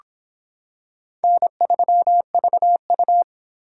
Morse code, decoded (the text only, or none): N3VU